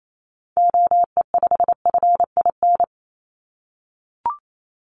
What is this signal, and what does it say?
Morse code, heard as a keyed tone, about 28 wpm, 700 Hz. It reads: OE5FIN
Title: Call Signs - 28wpm